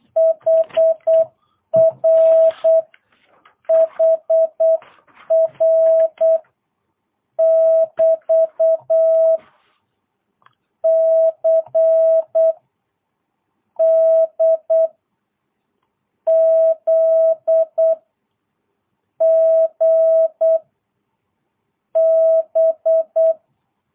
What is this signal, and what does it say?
Morse code, heard as a keyed tone, about 8 wpm, 650 Hz. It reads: HR HR <BT> C D Z G B
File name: jh8cbh / 2022cw